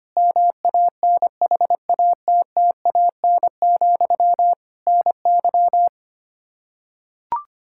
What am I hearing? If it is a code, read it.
Morse code, heard as a keyed tone, about 25 wpm, 700 Hz. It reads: MANHATTAN, NY
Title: Practice QSO - Exchange 1 - Beginning - 25wpm